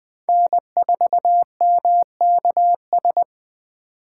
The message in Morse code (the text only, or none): N4MKS